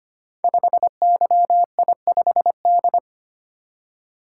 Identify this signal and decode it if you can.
Morse code, heard as a keyed tone, about 25 wpm, 700 Hz. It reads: 5YI5D
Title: Code Groups - 5 Letter:Numbers - 25wpm